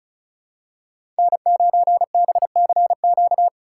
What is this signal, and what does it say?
Morse code, heard as a keyed tone, about 36 wpm, 700 Hz. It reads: N9BCQ
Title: USA Call Signs - 5 Characters - 4x Character Spacing - 35wpm